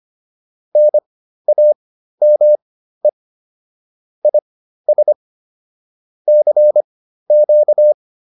Morse code, heard as a keyed tone, about 25 wpm, spaced as for 13 wpm, 600 Hz. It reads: NAME IS CQ
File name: QSO 108 13